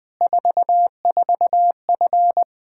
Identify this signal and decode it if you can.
Morse code, heard as a keyed tone, about 20 wpm, 700 Hz. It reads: 44F
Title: Practice QSO - Exchange 2 - Info - 20wpm